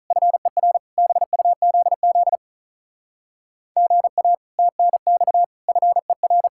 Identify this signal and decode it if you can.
Morse code, heard as a keyed tone, about 44 wpm, 700 Hz. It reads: FER BUZZ GA TNX FER